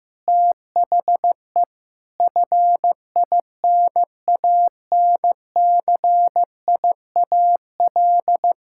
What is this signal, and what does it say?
Morse code, heard as a keyed tone, about 16 wpm, 700 Hz. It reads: THE FINANCIAL
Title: Sentences from Top 1000 Words - Review Part 5 - 15wpm